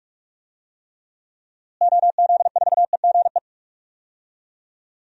Morse code, heard as a keyed tone, about 46 wpm, 700 Hz. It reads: OZ4EGE
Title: DXCC-100 Call Signs - 6 Characters - Repeated 3x - 45wpm